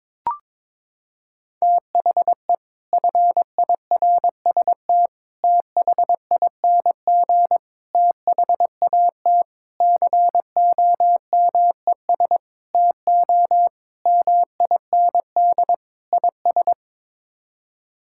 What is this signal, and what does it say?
Morse code, heard as a keyed tone, about 23 wpm, 700 Hz. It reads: THE FIRST THING THAT COMES TO MIND IS
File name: practice / 22wpm / Sets of 8 Words - 22wpm